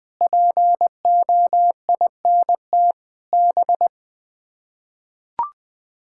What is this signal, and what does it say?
Morse code, heard as a keyed tone, about 20 wpm, 700 Hz. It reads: POINT B
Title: Sets of 8 Words - 20wpm